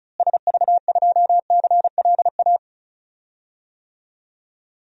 Morse code, heard as a keyed tone, about 36 wpm, 700 Hz. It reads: SV2CLA
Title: DXCC-100 Call Signs - 6 Characters - Repeated 2x - 35wpm